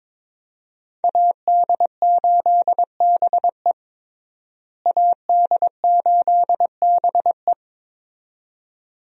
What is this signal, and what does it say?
Morse code, heard as a keyed tone, about 22 wpm, 700 Hz. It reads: AD8BE AD8BE
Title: USA Call Signs - 5 Characters - Repeated 2x - 22wpm